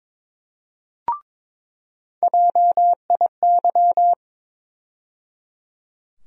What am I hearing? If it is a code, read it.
Morse code, heard as a keyed tone, about 22 wpm, 700 Hz. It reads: JIY
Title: USA Call Sign Postfixes - 3 Letters 22wpm